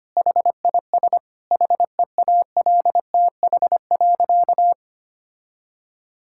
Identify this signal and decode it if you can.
Morse code, heard as a keyed tone, about 25 wpm, 700 Hz. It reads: HIS HEALTH.
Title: Sentences from Top 1000 Words - Review Part 6 - 25wpm